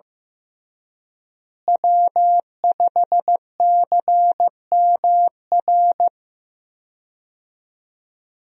Morse code, heard as a keyed tone, about 15 wpm, 700 Hz. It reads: W5CMR
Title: USA Call Signs - 5 Characters - Repeated 3x - 15wpm